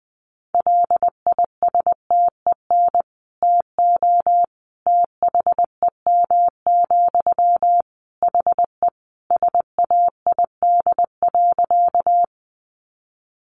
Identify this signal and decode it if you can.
Morse code, heard as a keyed tone, about 19 wpm, 700 Hz. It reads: LISTEN TO THEM, HE SAID.
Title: Sentences from Top 400 Words 20wpm